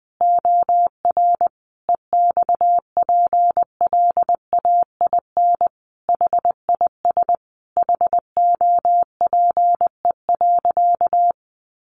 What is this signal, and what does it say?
Morse code, heard as a keyed tone, about 20 wpm, 700 Hz. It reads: OR EXPLAIN HIS HOPE.
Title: Sentences from Top 1100 Words 20wpm